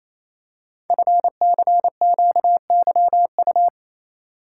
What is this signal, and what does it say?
Morse code, heard as a keyed tone, about 29 wpm, 700 Hz. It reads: FCQYU
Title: Code Groups - 5 Letters - 28wpm